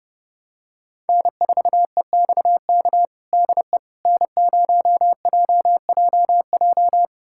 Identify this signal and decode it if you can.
Morse code, heard as a keyed tone, about 32 wpm, 700 Hz. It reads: N4EXK DE N0JJJ
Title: USA Callsign DE Callsign - 5 Characters - 30wpm